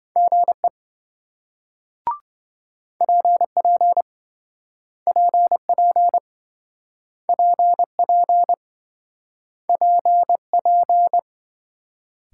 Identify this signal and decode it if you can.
Morse code, heard as a keyed tone, about 27 wpm, 700 Hz. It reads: GE PP PP PP PP